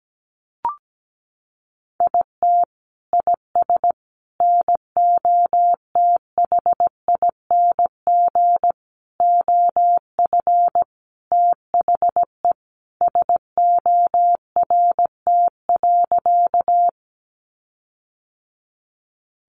Morse code, heard as a keyed tone, about 17 wpm, 700 Hz. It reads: IT IS NOTHING OF THE SORT.
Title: Sentences from Top 1300 Words 17wpm